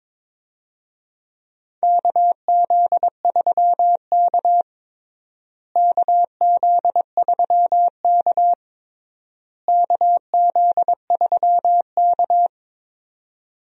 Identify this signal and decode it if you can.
Morse code, heard as a keyed tone, about 22 wpm, 700 Hz. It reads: KZ3K KZ3K KZ3K
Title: USA Call Signs - 4 Characters - Repeated 3x - 22wpm